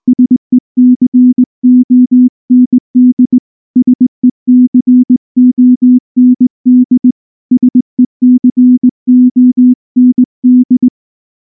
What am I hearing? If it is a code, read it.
Morse code, heard as a keyed tone, about 18 wpm, 250 Hz. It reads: SECOND SECOND SECOND